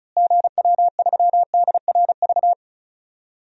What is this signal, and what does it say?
Morse code, heard as a keyed tone, about 34 wpm, 700 Hz. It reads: GW3DRV
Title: DXCC-100 Call Signs - 6 Characters - Repeated 3x - 35wpm